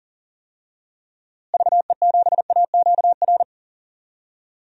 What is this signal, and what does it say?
Morse code, heard as a keyed tone, about 40 wpm, 700 Hz. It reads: VE7AQR
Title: DXCC-100 Call Signs - 6 Characters - Repeated 2x - 40wpm